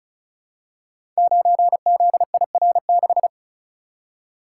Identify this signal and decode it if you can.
Morse code, heard as a keyed tone, about 36 wpm, 690 Hz. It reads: 9ZIR6
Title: Code Groups - 5 Letter:Numbers - 35wpm